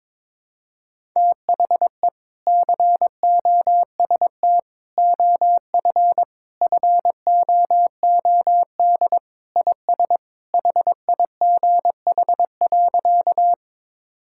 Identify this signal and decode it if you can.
Morse code, heard as a keyed tone, about 22 wpm, 700 Hz. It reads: THE COST OF FOOD IS HIGH.
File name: practice / 22wpm / Sentences with 3-Word Phrases - 22wpm